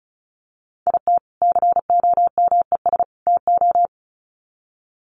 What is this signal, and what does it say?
Morse code, heard as a keyed tone, about 34 wpm, 700 Hz. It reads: IT COMES TO
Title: Sets of 3 Words 35wpm